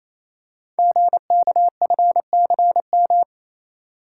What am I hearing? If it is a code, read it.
Morse code, heard as a keyed tone, about 28 wpm, 700 Hz. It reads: GKFCM
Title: Code Groups - 5 Letters - 28wpm